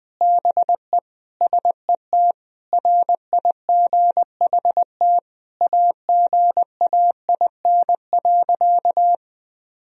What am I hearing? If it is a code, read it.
Morse code, heard as a keyed tone, about 20 wpm, 700 Hz. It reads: BE SET RIGHT AGAIN.
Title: Sentences from Top 1000 Words - Review Part 1 - 20wpm